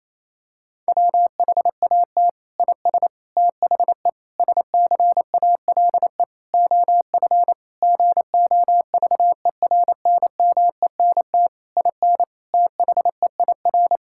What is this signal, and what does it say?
Morse code, heard as a keyed tone, about 28 wpm, 700 Hz. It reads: WHAT IS THE SCALE OF GOVERNMENT IN THEIR
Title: Sentences from Top 1000 Words - Review Part 5 - 28wpm